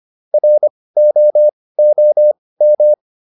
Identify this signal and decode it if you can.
Morse code, heard as a keyed tone, about 25 wpm, spaced as for 18 wpm, 600 Hz. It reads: ROOM